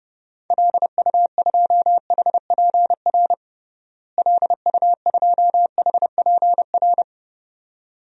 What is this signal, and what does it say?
Morse code, heard as a keyed tone, about 30 wpm, 700 Hz. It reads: LU2HPR LU2HPR
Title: DXCC-100 Call Signs - 6 Characters - Repeated 2x - 30wpm